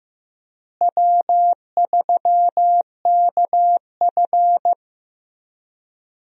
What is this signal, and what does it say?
Morse code, heard as a keyed tone, about 15 wpm, 700 Hz. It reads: W3KF
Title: USA Call Signs - 4 Characters - Repeated 3x - 15wpm